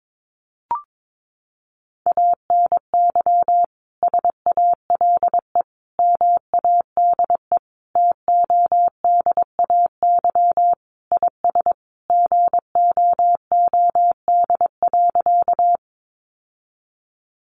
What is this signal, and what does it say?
Morse code, heard as a keyed tone, about 22 wpm, 700 Hz. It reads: ANY SALE MADE TODAY IS GOOD.
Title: Sentences from Top 1400 Words 22wpm